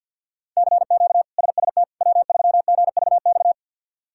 Morse code, heard as a keyed tone, about 46 wpm, 700 Hz. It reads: CQ SST W3GVX
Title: K1USN SST QSO Practice - 50wpm